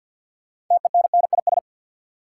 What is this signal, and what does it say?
Morse code, heard as a keyed tone, about 52 wpm, 700 Hz. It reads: TENNIS
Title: Six Letter Words - 50wpm